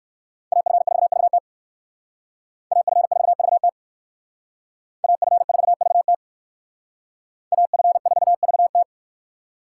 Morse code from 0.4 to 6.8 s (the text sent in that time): AF4VT AF4VT AF4VT